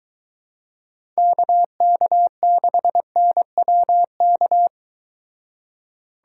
Morse code, heard as a keyed tone, about 22 wpm, 700 Hz. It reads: KK6NWK